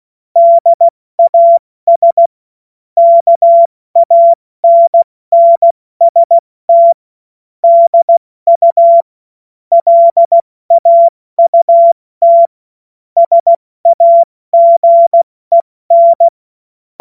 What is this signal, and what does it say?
Morse code, heard as a keyed tone, about 16 wpm, 680 Hz. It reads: DAS KANNST DU LAUT SAGEN
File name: Morsekurs / Witze / Witz 28. März 2023 70 F80